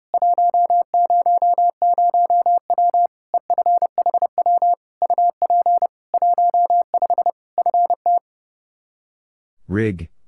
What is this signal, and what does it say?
Morse code, heard as a keyed tone, about 32 wpm, 700 Hz. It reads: 100W EFHW UP 15 FT
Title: Practice QSO - Exchange 2 - Info - 30wpm